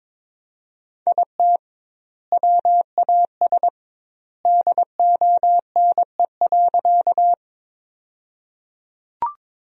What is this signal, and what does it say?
Morse code, heard as a keyed tone, about 22 wpm, 700 Hz. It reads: IT WAS DONE.
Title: 2x Word-Spacing - Sentences from Top 500 Words - 22wpm